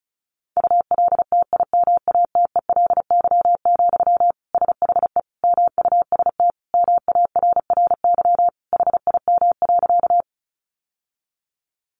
Morse code, heard as a keyed tone, about 34 wpm, 700 Hz. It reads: ULTIMATELY, SHE MUST MARRY HIM.
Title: Sentences from Top 1200 Words 35wpm